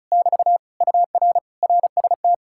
Morse code, heard as a keyed tone, about 48 wpm, spaced as for 42 wpm, 700 Hz. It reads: <BT> UR RST
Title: Speed-Racing Common QSO Phrases Encore - 35wpm